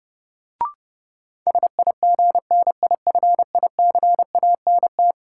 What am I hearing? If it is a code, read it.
Morse code, heard as a keyed tone, about 30 wpm, 700 Hz. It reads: SIGNIFICANT
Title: Sentences from Top 1000 Words - Review Part 6 - New Word Drill - 30wpm